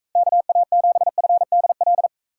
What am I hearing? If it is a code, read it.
Morse code, heard as a keyed tone, about 42 wpm, 700 Hz. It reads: KA7FDL